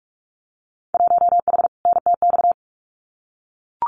Morse code, heard as a keyed tone, about 46 wpm, 700 Hz. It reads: 1H NTX
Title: Winter Field Day Exchanges 45wpm